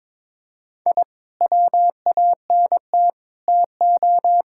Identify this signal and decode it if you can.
Morse code, heard as a keyed tone, about 22 wpm, 700 Hz. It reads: I WANT TO
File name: Sets of 9 Words - 22wpm